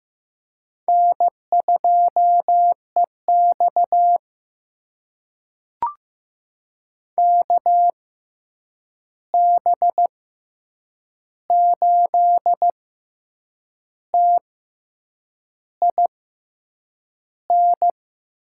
Morse code, heard as a keyed tone, about 15 wpm, spaced as for 7 wpm, 700 Hz. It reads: N2EX KB8TIN
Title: USA Call Signs - 6x Character Spacing - 15wpm